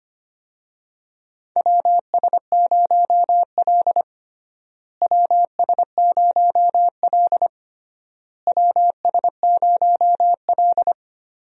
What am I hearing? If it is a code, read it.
Morse code, heard as a keyed tone, about 25 wpm, 700 Hz. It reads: WS0L WS0L WS0L
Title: USA Call Signs - 4 Characters - Repeated 3x - 25wpm